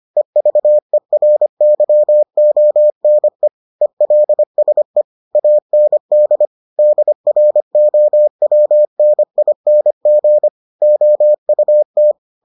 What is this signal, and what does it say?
Morse code, heard as a keyed tone, about 25 wpm, 600 Hz. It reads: EVERYONE ELSE AND DROWNING OUT